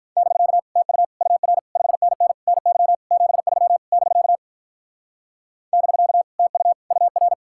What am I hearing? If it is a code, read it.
Morse code, heard as a keyed tone, about 43 wpm, 700 Hz. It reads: <BK> TU UR 5NN NY 73 <BK> <BK> TU UR
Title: Speed-Racing Common QSO Phrases - 35wpm